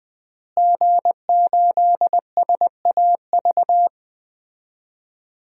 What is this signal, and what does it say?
Morse code, heard as a keyed tone, about 20 wpm, 700 Hz. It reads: G8SAV
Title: DXCC-100 Call Signs - 5 Characters - Repeated 2x - 20wpm